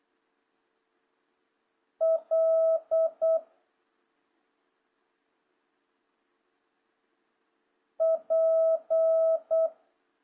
Morse code, heard as a keyed tone, about 8 wpm, 650 Hz. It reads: L P
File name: jh8cbh / 2026cw